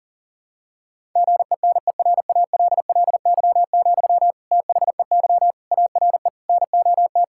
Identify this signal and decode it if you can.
Morse code, heard as a keyed tone, about 37 wpm, 700 Hz. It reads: GENERALLY, THEY ARE NOT
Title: Sentences from Top 1000 Words - Review Part 6 - 40wpm